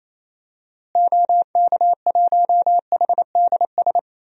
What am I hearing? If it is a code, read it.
Morse code, heard as a keyed tone, about 28 wpm, 700 Hz. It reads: OK1HDS
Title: DXCC-100 Call Signs - 6 Characters - Repeated 3x - 28wpm